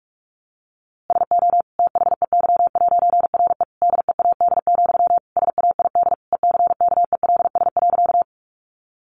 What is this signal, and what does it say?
Morse code, heard as a keyed tone, about 44 wpm, 700 Hz. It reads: SO THEY'RE DEAD, SAID ECKELS.
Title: A Sound Of Thunder 45wpm